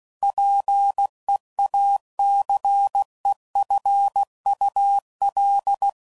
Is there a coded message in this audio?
Yes: Morse code from 0.2 to 5.9 s, heard as a keyed tone, about 16 wpm, 800 Hz. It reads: PEACEFUL